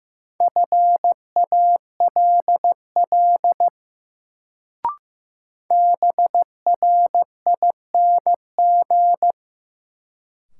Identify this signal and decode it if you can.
Morse code, heard as a keyed tone, about 15 wpm, 700 Hz. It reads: FALL BRING
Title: Sentences from Top 500 Words - New Word Drill - 15wpm